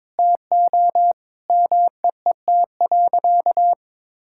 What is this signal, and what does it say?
Morse code, heard as a keyed tone, about 21 wpm, 700 Hz. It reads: TO MEET.
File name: practice / 22wpm / Sentences with 3-Word Phrases - 22wpm